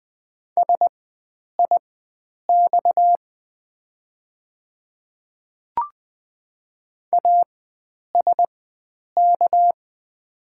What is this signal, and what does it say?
Morse code, heard as a keyed tone, about 20 wpm, spaced as for 9 wpm, 700 Hz. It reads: SIX ASK
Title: 4x Character Spacing - Top 500 Words - 20wpm